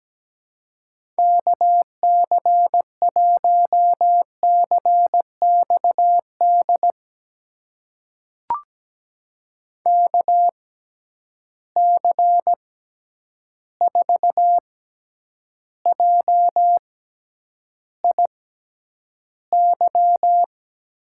Morse code, heard as a keyed tone, about 17 wpm, spaced as for 9 wpm, 700 Hz. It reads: KC1CXD KC4JIY
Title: USA Call Signs - 6 Characters - 6x Character Spacing - 17wpm